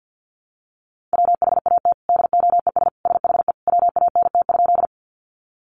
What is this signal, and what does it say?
Morse code, heard as a keyed tone, about 50 wpm, 700 Hz. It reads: WHAT DOES SHE WANT?
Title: Sentences from Top 200 Words 50wpm